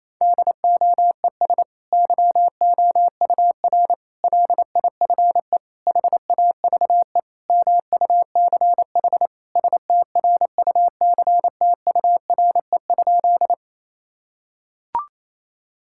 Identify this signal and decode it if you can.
Morse code, heard as a keyed tone, about 28 wpm, 700 Hz. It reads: DOES YOUR LIFE HAVE MUCH STRUCTURE?
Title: Sentences from Top 1000 Words - Review Part 6 - 28wpm